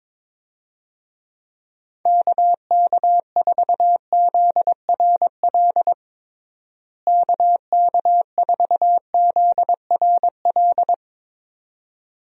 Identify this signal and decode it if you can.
Morse code, heard as a keyed tone, about 22 wpm, 700 Hz. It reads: KK4ZRL KK4ZRL
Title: USA Call Signs - 6 Characters - Repeated 2x - 22wpm